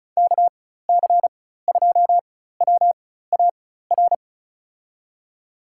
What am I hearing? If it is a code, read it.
Morse code, heard as a keyed tone, about 36 wpm, spaced as for 16 wpm, 700 Hz. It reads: KC2WAR
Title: USA Call Signs - 4x Character Spacing - 35wpm